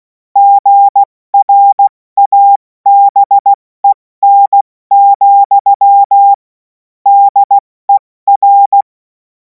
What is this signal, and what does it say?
Morse code, heard as a keyed tone, about 16 wpm, 800 Hz. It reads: GRABEN, DER
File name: Morsekurs / Witze / Witz 17. Februar 2026 70 F80